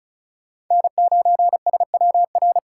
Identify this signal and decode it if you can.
Morse code, heard as a keyed tone, about 36 wpm, 700 Hz. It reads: N9SWR
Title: USA Call Signs - 4x Character Spacing - 35wpm